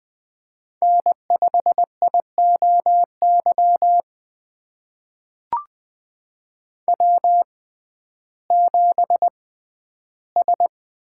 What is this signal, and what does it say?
Morse code, heard as a keyed tone, about 20 wpm, spaced as for 12 wpm, 700 Hz. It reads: N5IOY W7S
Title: USA Call Signs - 5 Characters - 6x Character Spacing - 20wpm